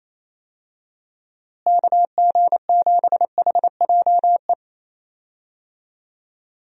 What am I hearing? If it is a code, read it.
Morse code, heard as a keyed tone, about 28 wpm, 700 Hz. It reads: KG7HJE
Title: USA Call Signs - 6 Characters - Repeated 3x - 28wpm